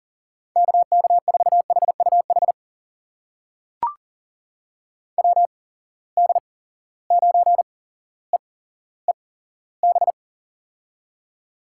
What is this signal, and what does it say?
Morse code, heard as a keyed tone, about 40 wpm, spaced as for 17 wpm, 700 Hz. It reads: KK4HUH WD9EEB
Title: USA Call Signs - 6 Characters - 8x Character Spacing - 40wpm